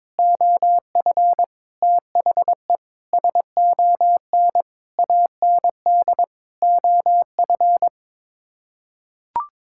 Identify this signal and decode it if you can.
Morse code, heard as a keyed tone, about 22 wpm, 700 Hz. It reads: OF THE SON AND OF
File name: Sets of 9 Words - 22wpm